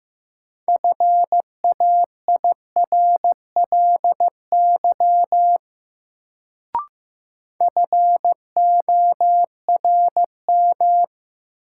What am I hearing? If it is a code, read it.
Morse code, heard as a keyed tone, about 16 wpm, 700 Hz. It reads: FAIRLY FORM
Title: Sentences from Top 1100 Words - New Word Drill Alt - 15wpm